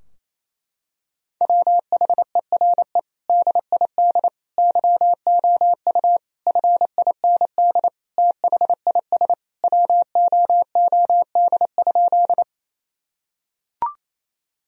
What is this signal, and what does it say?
Morse code, heard as a keyed tone, about 27 wpm, 700 Hz. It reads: WHERE DID YOU FIND THIS WOOD?
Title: Sentences from Top 1000 Words - Review Part 2 - 28wpm